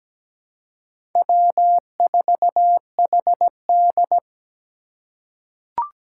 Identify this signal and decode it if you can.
Morse code, heard as a keyed tone, about 17 wpm, 700 Hz. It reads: W4HD
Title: Four Challenging Letter-Number Combinations 17wpm